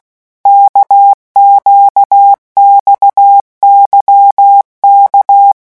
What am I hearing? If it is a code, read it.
Morse code, heard as a keyed tone, about 16 wpm, 800 Hz. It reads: KQXYK
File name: morse mp3 / revision1-vitesse 15 mots